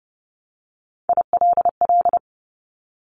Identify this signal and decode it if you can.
Morse code, heard as a keyed tone, about 30 wpm, 700 Hz. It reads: ILL
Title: Top Three Letter Combinations in Words 30wpm